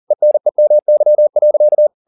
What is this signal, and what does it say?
Morse code, heard as a keyed tone, about 40 wpm, 600 Hz. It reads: ENEMY.
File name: the old man and the sea 2 40wpm